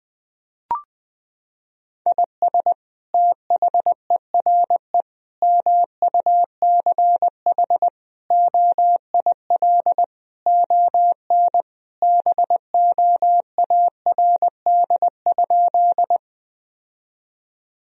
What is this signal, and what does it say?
Morse code, heard as a keyed tone, about 20 wpm, 700 Hz. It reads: IS THERE MUCH OIL ON BOARD?
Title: Sentences from Top 1000 Words - Review Part 4 - 20wpm